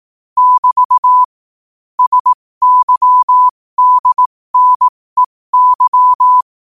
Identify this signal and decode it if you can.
Morse code, heard as a keyed tone, about 18 wpm, spaced as for 15 wpm, 1000 Hz. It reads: <BT> SYDNEY